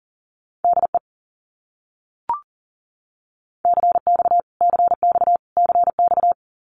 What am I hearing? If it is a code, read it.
Morse code, heard as a keyed tone, about 40 wpm, 700 Hz. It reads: DE CX CX CX
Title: Abbreviations Repeated 3x - 40wpm